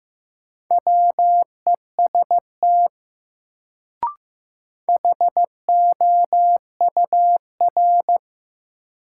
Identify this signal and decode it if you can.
Morse code, heard as a keyed tone, about 15 wpm, 700 Hz. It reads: WEST HOUR
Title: Sentences from Top 400 Words - New Word Drill - 15wpm